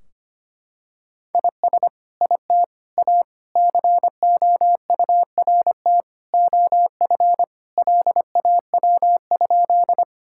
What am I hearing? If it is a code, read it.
Morse code, heard as a keyed tone, about 25 wpm, 700 Hz. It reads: IS IT A COURT OF LAW?
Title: Sentences from Top 1600 Words 25wpm